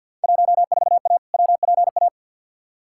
Morse code, heard as a keyed tone, about 52 wpm, 700 Hz. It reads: 14A WPA
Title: Field Day Exchanges Encore - 50wpm